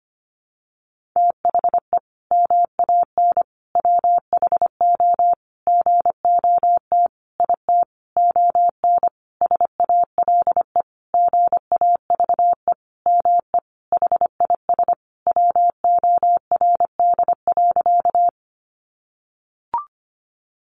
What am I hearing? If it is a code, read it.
Morse code, heard as a keyed tone, about 25 wpm, 700 Hz. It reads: THE MAN WHO GOT IT ON SALE GAVE ME HIS WORD.
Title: Sentences from Top 1400 Words 25wpm